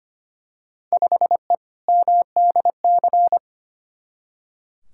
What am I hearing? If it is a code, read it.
Morse code, heard as a keyed tone, about 25 wpm, 700 Hz. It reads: 5E MDC
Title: Field Day Exchanges - 25wpm